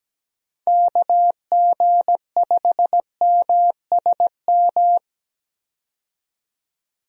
Morse code, heard as a keyed tone, about 17 wpm, 700 Hz. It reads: KG5MSM
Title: USA Call Signs - 6 Characters - 17wpm